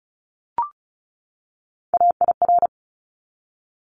Morse code, heard as a keyed tone, about 34 wpm, 700 Hz. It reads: AIR
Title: Top 200 Words 35wpm